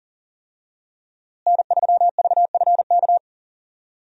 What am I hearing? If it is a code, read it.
Morse code, heard as a keyed tone, about 40 wpm, 700 Hz. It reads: N3VFK
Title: USA Call Signs - 5 Characters - Repeated 2x - 40wpm